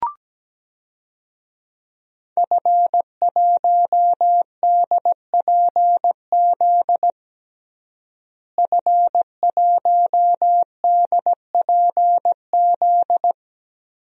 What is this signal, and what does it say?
Morse code, heard as a keyed tone, about 17 wpm, 700 Hz. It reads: F1DPZ F1DPZ
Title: DXCC-100 Call Signs - 5 Characters - Repeated 3x - 17wpm